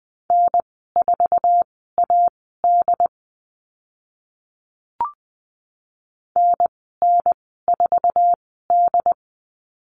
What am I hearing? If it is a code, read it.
Morse code, heard as a keyed tone, about 20 wpm, spaced as for 14 wpm, 700 Hz. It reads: N4AD NN4D
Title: 082 Lesson 14g - Number 2 - Call Signs 20wpm